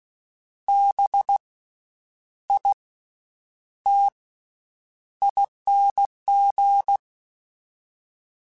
Morse code, heard as a keyed tone, about 16 wpm, spaced as for 8 wpm, 800 Hz. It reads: BITING
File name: Special INGs / Special INGs-mp3s / VerySlowToFaster Ing @15